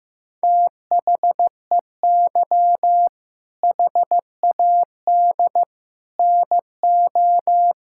Morse code, heard as a keyed tone, about 14 wpm, 700 Hz. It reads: THEY HAD NO
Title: Sentences from Top 1000 Words - Review Part 6 - 15wpm